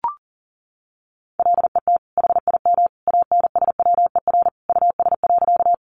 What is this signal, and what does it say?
Morse code, heard as a keyed tone, about 40 wpm, 700 Hz. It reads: LET HIM ANSWER US.
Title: Sentences from Top 300 Words 40wpm